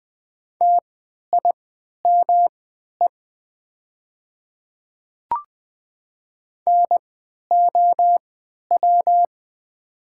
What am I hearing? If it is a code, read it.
Morse code, heard as a keyed tone, about 20 wpm, spaced as for 11 wpm, 700 Hz. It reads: TIME NOW
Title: 3x Character Spacing - Top 200 Words - 20wpm